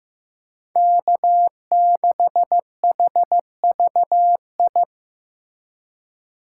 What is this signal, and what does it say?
Morse code, heard as a keyed tone, about 15 wpm, 700 Hz. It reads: K6HVI
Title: USA Call Signs - 5 Characters - 6x Character Spacing - 15wpm